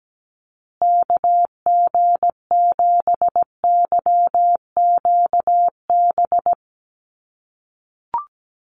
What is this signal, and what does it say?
Morse code, heard as a keyed tone, about 17 wpm, 700 Hz. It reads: KG7YQB
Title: USA Call Signs Repeated 2x 17wpm